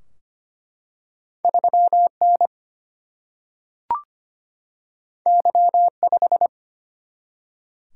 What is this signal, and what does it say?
Morse code, heard as a keyed tone, about 25 wpm, 700 Hz. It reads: 3N Y5
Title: Single-Letter Single-Number 25wpm